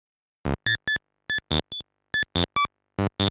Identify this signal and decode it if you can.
There is no code.